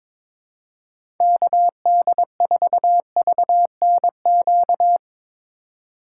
Morse code, heard as a keyed tone, about 22 wpm, 700 Hz. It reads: KD4VNQ
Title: USA Call Signs - 6 Characters - 4x Character Spacing - 22wpm